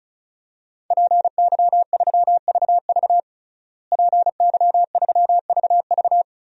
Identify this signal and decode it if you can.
Morse code, heard as a keyed tone, about 36 wpm, 700 Hz. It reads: PY3VV PY3VV
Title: DXCC-100 Call Signs - 5 Characters - Repeated 2x - 35wpm